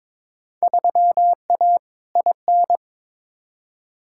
Morse code, heard as a keyed tone, about 22 wpm, 700 Hz. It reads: 3A IN
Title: Field Day Exchanges - 22wpm